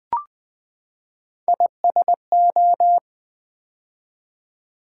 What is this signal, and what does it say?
Morse code, heard as a keyed tone, about 20 wpm, 700 Hz. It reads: ISO
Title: Build A Word - 3 Letters At A Time - 9-Letter Words - 20wpm